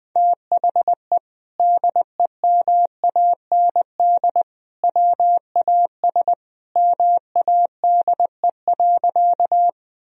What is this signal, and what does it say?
Morse code, heard as a keyed tone, about 20 wpm, 700 Hz. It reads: THE DEMAND WAS MADE.
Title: Sentences from Top 1000 Words - Review Part 4 - 20wpm